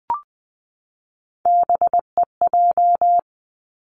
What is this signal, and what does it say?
Morse code, heard as a keyed tone, about 20 wpm, 700 Hz. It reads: BEJ
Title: Three Challenging Letter-Number Combinations 20wpm